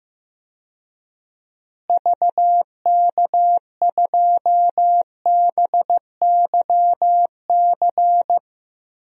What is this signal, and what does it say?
Morse code, heard as a keyed tone, about 16 wpm, 700 Hz. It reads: VK2BYC
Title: DXCC-100 Call Signs - 6 Characters - Repeated 3x - 15wpm